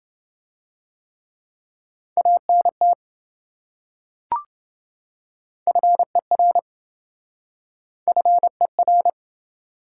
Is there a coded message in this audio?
Yes: Morse code from 2.2 to 9.1 s, heard as a keyed tone, about 26 wpm, 700 Hz. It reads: ANT FER FER